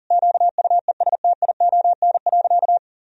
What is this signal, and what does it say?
Morse code, heard as a keyed tone, about 40 wpm, 700 Hz. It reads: QUESTION.
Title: Sentences from Top 1000 Words - Review Part 1 - 40wpm